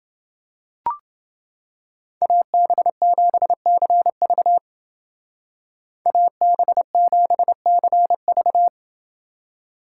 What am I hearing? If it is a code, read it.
Morse code, heard as a keyed tone, about 28 wpm, 700 Hz. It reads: AB7CV AB7CV